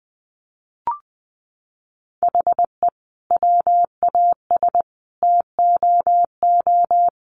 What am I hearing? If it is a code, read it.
Morse code, heard as a keyed tone, about 20 wpm, 700 Hz. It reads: HE WAS TOO
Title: Sentences from Top 600 Words 20wpm